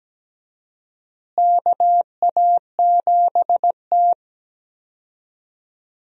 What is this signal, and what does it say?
Morse code, heard as a keyed tone, about 17 wpm, 700 Hz. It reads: KA7T
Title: USA Call Signs - 4 Characters - Repeated 2x - 17wpm